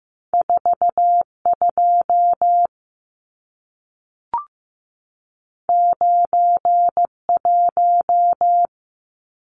Morse code, heard as a keyed tone, about 15 wpm, 700 Hz. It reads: 42 91
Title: Numbers - 2-Digit 15wpm